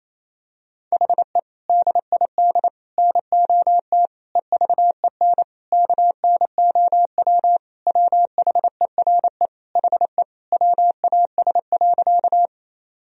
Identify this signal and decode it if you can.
Morse code, heard as a keyed tone, about 28 wpm, 700 Hz. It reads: HE DID NOT EVEN KNOW WHERE HE WAS.
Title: Sentences from Top 1000 Words - Review Part 1 - 28wpm